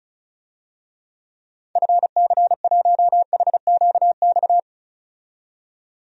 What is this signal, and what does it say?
Morse code, heard as a keyed tone, about 36 wpm, 700 Hz. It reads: FC1HQX
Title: DXCC-100 Call Signs - 6 Characters - 35wpm